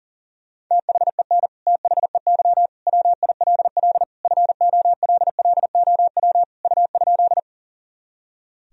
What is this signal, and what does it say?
Morse code, heard as a keyed tone, about 37 wpm, 700 Hz. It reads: THEN THEY WILL FOLLOW U?
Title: Sentences from Top 200 Words and Abbreviations 40wpm